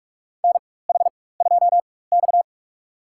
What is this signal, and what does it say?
Morse code, heard as a keyed tone, about 41 wpm, spaced as for 20 wpm, 700 Hz. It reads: NH2X
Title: USA Call Signs - 4x Character Spacing - 45wpm